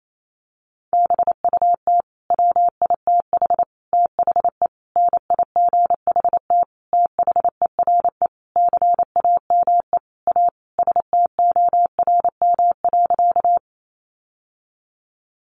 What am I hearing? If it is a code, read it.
Morse code, heard as a keyed tone, about 28 wpm, 700 Hz. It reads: BUT WITH THE NIGHT THERE CAME A STORM.
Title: Sentences from Top 1500 Words 28wpm